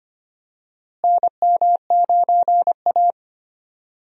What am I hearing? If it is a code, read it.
Morse code, heard as a keyed tone, about 25 wpm, 700 Hz. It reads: NM9A
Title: USA Call Signs - 4 Characters - Repeated 3x - 25wpm